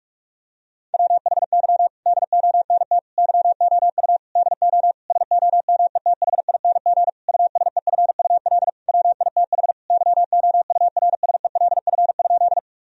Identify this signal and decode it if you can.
Morse code, heard as a keyed tone, about 46 wpm, 700 Hz. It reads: WHY DONT YOU DO SOMETHING USEFUL WITH YOURSELF?
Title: Sentences from Top 1000 Words - Review Part 4 - 45wpm